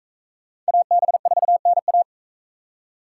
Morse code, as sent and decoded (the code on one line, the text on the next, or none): .- -... ....- -. .-
AB4NA